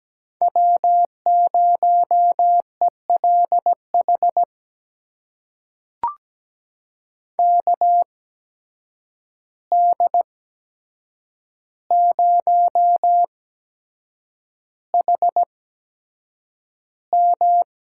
Morse code, heard as a keyed tone, about 17 wpm, spaced as for 7 wpm, 700 Hz. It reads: W0ELH KD0HM
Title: USA Call Signs - 5 Characters - 8x Character Spacing - 17wpm